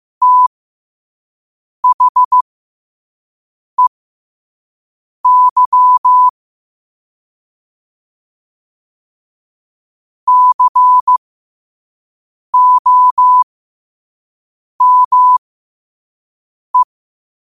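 Morse code, heard as a keyed tone, about 15 wpm, spaced as for 5 wpm, 1000 Hz. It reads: THEY COME
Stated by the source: Morse code at 5 wpm.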